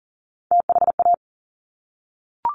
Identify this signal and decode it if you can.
Morse code, heard as a keyed tone, about 42 wpm, 690 Hz. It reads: THA
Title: Top Three Letter Combinations in Words 40wpm